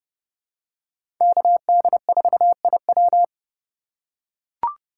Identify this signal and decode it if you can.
Morse code, heard as a keyed tone, about 31 wpm, 700 Hz. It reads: KD4IW
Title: USA Call Signs - 5 Characters - 3x Character Spacing - 30wpm